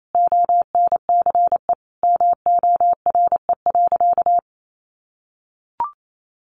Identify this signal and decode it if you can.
Morse code, heard as a keyed tone, about 28 wpm, 700 Hz. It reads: ONCE MORE.
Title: Sentences from Top 1100 Words 28wpm